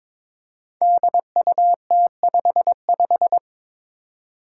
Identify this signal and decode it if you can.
Morse code, heard as a keyed tone, about 22 wpm, 700 Hz. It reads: DUT55
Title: Five Challenging Letter-Number Combinations 22wpm